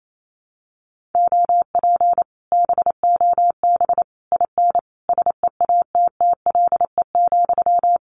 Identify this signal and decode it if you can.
Morse code, heard as a keyed tone, about 28 wpm, 700 Hz. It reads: OP BOB IN SEATTLE,
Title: Common QSO Phrases Encore 28wpm